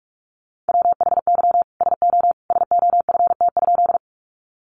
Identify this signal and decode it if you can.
Morse code, heard as a keyed tone, about 44 wpm, 700 Hz. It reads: WHY SO SOFT?
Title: Sentences from Top 1500 Words 45wpm